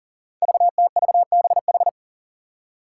Morse code, heard as a keyed tone, about 40 wpm, 700 Hz. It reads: VTVBH
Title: Five Challenging Letter-Number Combinations 40wpm